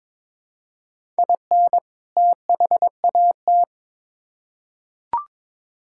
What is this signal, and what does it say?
Morse code, heard as a keyed tone, about 22 wpm, 700 Hz. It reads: IN THAT
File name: Build A Phrase - 2-Word Phrases - 22wpm